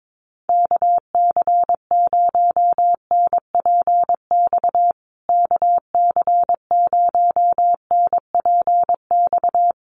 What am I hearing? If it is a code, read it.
Morse code, heard as a keyed tone, about 22 wpm, 700 Hz. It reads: KC0NPX KC0NPX
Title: USA Call Signs Repeated 2x 22wpm